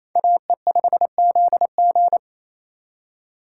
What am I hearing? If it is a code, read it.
Morse code, heard as a keyed tone, about 36 wpm, 700 Hz. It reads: AE5ZG